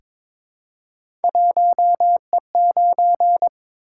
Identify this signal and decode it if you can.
Morse code, heard as a keyed tone, about 23 wpm, 700 Hz. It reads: 1E9
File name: practice / 22wpm / Three Letter-Number Combinations - 22wpm